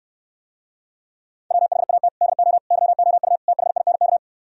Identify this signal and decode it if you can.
Morse code, heard as a keyed tone, about 68 wpm, 700 Hz. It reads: WHAT DO YOU THINK